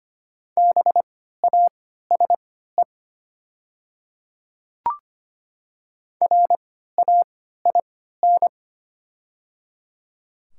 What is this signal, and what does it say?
Morse code, heard as a keyed tone, about 25 wpm, spaced as for 14 wpm, 700 Hz. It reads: BASE RAIN
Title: 3x Character Spacing - Top 500 Words - 25wpm